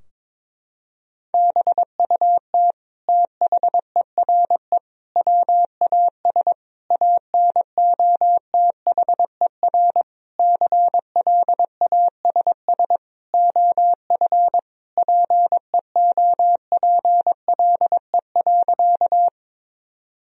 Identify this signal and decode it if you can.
Morse code, heard as a keyed tone, about 22 wpm, 700 Hz. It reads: BUT THERE WAS ANOTHER CLASS OF PEOPLE.
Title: Sentences from Top 1000 Words - Review Part 2 - 22wpm